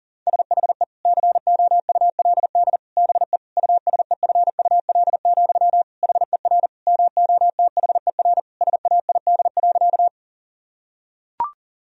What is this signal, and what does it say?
Morse code, heard as a keyed tone, about 40 wpm, 700 Hz. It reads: SHE COULD BE USEFUL, HER MOTHER SAID.
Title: Sentences from Top 1000 Words - Review Part 4 - 40wpm